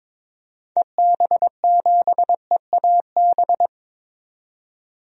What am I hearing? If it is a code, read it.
Morse code, heard as a keyed tone, about 22 wpm, 700 Hz. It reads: EB7EAB
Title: DXCC-100 Call Signs - 6 Characters - 22wpm